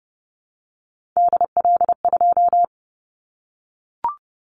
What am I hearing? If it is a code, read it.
Morse code, heard as a keyed tone, about 30 wpm, 700 Hz. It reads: DL2
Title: Callsign Prefixes for Top 25 DXCC - 30wpm